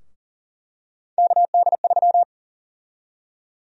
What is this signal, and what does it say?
Morse code, heard as a keyed tone, about 40 wpm, 700 Hz. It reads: KD3
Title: USA Call Sign Prefixes - 40wpm